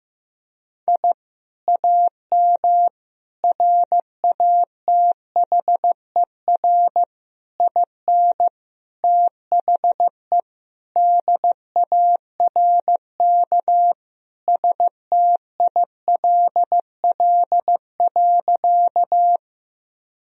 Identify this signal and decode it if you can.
Morse code, heard as a keyed tone, about 16 wpm, 700 Hz. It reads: I AM RATHER IN THE DARK STILL.
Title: Sentences from Top 1000 Words - Review Part 4 - 15wpm